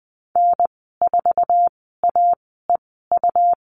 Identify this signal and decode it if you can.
Morse code, heard as a keyed tone, about 20 wpm, spaced as for 15 wpm, 700 Hz. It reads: N4AEU
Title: 128 Lesson 21g - Number 6 - Call Signs 20wpm